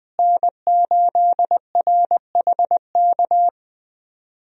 Morse code, heard as a keyed tone, about 20 wpm, 700 Hz. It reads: N8RHK
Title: USA Call Signs - 5 Characters - Repeated 3x - 20wpm